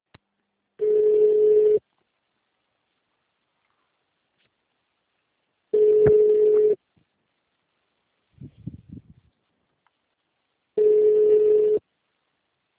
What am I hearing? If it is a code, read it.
Ringback tone (Europe, 425 Hz).
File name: out- -2011- - - .6991